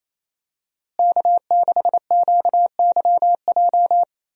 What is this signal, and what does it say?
Morse code, heard as a keyed tone, about 28 wpm, 700 Hz. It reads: K6QYJ
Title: USA Call Signs - 5 Characters - Repeated 3x - 28wpm